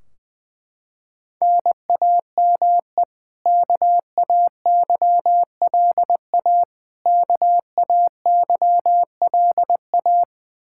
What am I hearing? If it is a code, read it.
Morse code, heard as a keyed tone, about 20 wpm, 700 Hz. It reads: NAME KAYLA KAYLA
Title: Practice QSO - Exchange 1 - Beginning - 20wpm